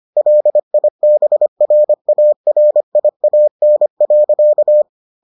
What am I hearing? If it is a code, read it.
Morse code, heard as a keyed tone, about 25 wpm, 600 Hz. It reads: LIBRARIAN.